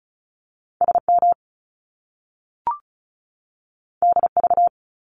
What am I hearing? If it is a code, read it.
Morse code, heard as a keyed tone, about 36 wpm, 700 Hz. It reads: SM DV
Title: Top Two Letter Combinations in Words 35wpm